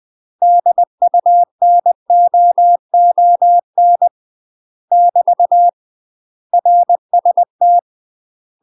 Morse code, heard as a keyed tone, about 20 wpm, 700 Hz. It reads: DUNOON <BT> RST
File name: UK MASTER QSOs TRAINING / 20WPM / UK MASTER QSOs 0007